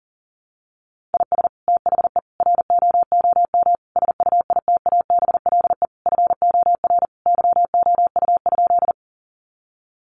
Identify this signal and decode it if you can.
Morse code, heard as a keyed tone, about 37 wpm, 700 Hz. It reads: IS THE ROOM SUITABLE FOR YOU?
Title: Sentences from Top 1400 Words 40wpm